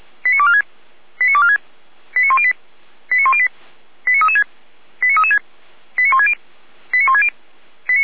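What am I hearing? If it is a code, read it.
Five-tone selective call (ZVEI), heard as quick runs of five tones, tones of about 70 ms. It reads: 89246 89246 89108 89108 89336 89336 89160 89160
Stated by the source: ZVEI five-tone call 89078 received by SDR.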